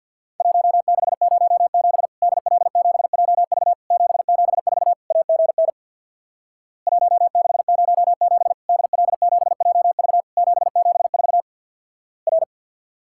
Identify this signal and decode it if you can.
Morse code, heard as a keyed tone, about 50 wpm, 700 Hz. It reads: 1607 DL7JV 774 AGN 1607 DL7JV 774 R
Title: Worked All Europe DX Contest - QTC Exchange with Repeats - 50wpm